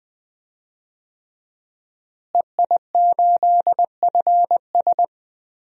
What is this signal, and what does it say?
Morse code, heard as a keyed tone, about 20 wpm, 700 Hz. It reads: EI8FS
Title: DXCC-100 Call Signs - 5 Characters - Repeated 3x - 20wpm